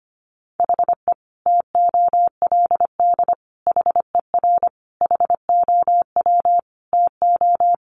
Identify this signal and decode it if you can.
Morse code, heard as a keyed tone, about 25 wpm, 700 Hz. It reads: HE TOLD HER HOW TO
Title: Sentences from Top 1500 Words 25wpm